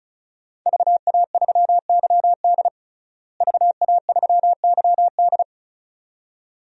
Morse code, heard as a keyed tone, about 36 wpm, 700 Hz. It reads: VA3YD VA3YD
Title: DXCC-100 Call Signs - 5 Characters - Repeated 2x - 35wpm